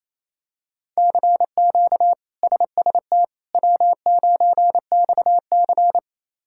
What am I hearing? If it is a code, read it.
Morse code, heard as a keyed tone, about 27 wpm, 700 Hz. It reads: CQ SST W9XC
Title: K1USN SST QSO Practice - 28wpm